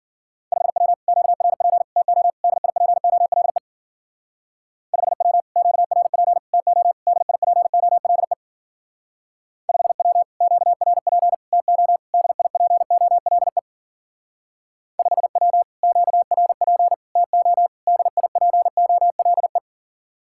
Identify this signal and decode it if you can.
Morse code, heard as a keyed tone, about 48 wpm, 700 Hz. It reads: 5W QRP TO DIPOLE 5W QRP TO DIPOLE 5W QRP TO DIPOLE 5W QRP TO DIPOLE
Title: Speed-Racing Common QSO Phrases - 40wpm